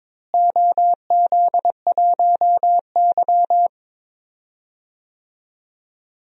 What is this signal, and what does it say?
Morse code, heard as a keyed tone, about 22 wpm, 700 Hz. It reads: OZ1Y
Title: DXCC-100 Call Signs - 4 Characters - Repeated 3x - 22wpm